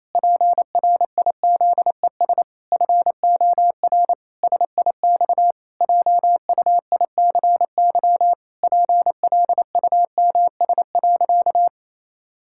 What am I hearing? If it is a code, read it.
Morse code, heard as a keyed tone, about 28 wpm, 700 Hz. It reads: PRIZES FOR SIX JUICY PLUMS.
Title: Pangram Challenge v2 - 28wpm